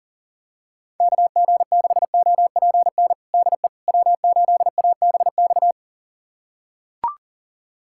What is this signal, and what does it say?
Morse code, heard as a keyed tone, about 40 wpm, 700 Hz. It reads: KG6OPN DE W8ABX
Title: USA Callsign DE Callsign 40wpm